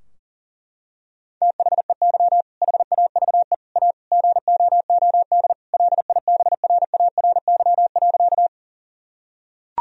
Morse code, heard as a keyed tone, about 37 wpm, 700 Hz. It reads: THEY HAVE A GOOD LIBRARY.
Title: Sentences from Top 1000 Words - Review Part 4 - 40wpm